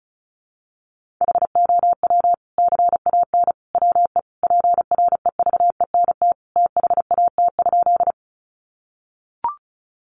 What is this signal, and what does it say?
Morse code, heard as a keyed tone, about 34 wpm, 700 Hz. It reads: HOW CAN WE PREVENT THAT?
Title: Sentences from Top 1100 Words 35wpm